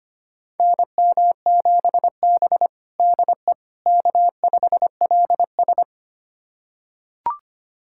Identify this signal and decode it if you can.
Morse code, heard as a keyed tone, about 25 wpm, 700 Hz. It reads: NM7B DE K5LS
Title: USA Callsign DE Callsign - 4 Characters - 25wpm